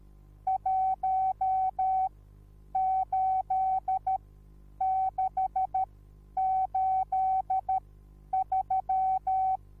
Morse code, heard as a keyed tone, about 13 wpm, spaced as for 9 wpm, 760 Hz. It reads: 18683